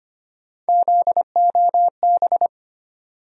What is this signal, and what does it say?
Morse code, heard as a keyed tone, about 25 wpm, 700 Hz. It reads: ZOB
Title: USA Call Sign Postfixes - 3 Letters 25wpm